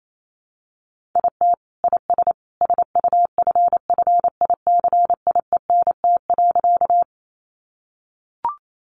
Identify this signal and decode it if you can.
Morse code, heard as a keyed tone, about 28 wpm, 700 Hz. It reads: IT IS SUFFICIENT.
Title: Sentences from Top 1400 Words 28wpm